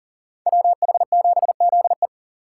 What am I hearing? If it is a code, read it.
Morse code, heard as a keyed tone, about 52 wpm, 700 Hz. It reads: WH7ZE